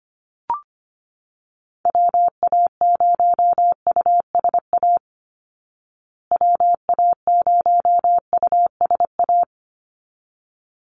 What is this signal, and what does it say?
Morse code, heard as a keyed tone, about 25 wpm, 700 Hz. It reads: WA0USA WA0USA
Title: Call Signs Repeated 3x - 25wpm